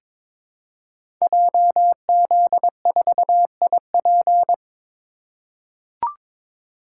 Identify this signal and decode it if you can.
Morse code, heard as a keyed tone, about 22 wpm, 700 Hz. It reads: JZ4IP
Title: Code Groups - 5 Letter:Numbers - 22wpm